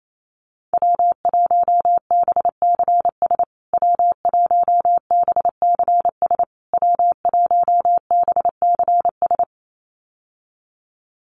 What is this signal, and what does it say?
Morse code, heard as a keyed tone, about 28 wpm, 700 Hz. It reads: W1BCS W1BCS W1BCS
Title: USA Call Signs Repeated 3x 28wpm